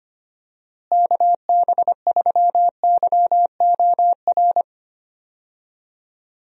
Morse code, heard as a keyed tone, about 25 wpm, 700 Hz. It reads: KB3YOR
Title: USA Call Signs - 6 Characters - Repeated 2x - 25wpm